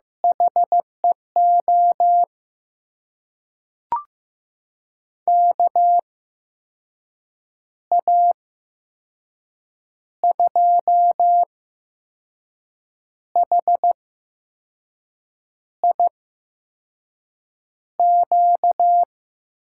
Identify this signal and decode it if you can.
Morse code, heard as a keyed tone, about 15 wpm, spaced as for 5 wpm, 700 Hz. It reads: HEO KA2HIQ
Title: USA Call Signs - 6 Characters - 8x Character Spacing - 15wpm